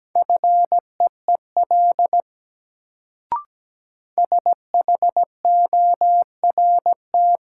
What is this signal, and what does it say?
Morse code, heard as a keyed tone, about 17 wpm, 700 Hz. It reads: FEEL SHORT
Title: Sentences from Top 400 Words - New Word Drill - 17wpm